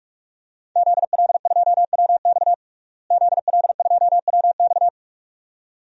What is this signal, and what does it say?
Morse code, heard as a keyed tone, about 46 wpm, 700 Hz. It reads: ZL2WX ZL2WX
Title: DXCC-100 Call Signs - 5 Characters - Repeated 2x - 45wpm